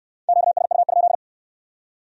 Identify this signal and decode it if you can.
Morse code, heard as a keyed tone, about 61 wpm, 700 Hz. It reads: QSU?